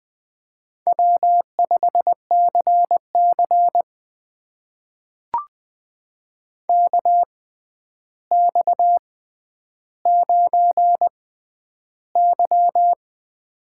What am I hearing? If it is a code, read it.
Morse code, heard as a keyed tone, about 20 wpm, spaced as for 10 wpm, 700 Hz. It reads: W5CC KX9Y
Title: USA Call Signs - 4 Characters - 6x Character Spacing - 20wpm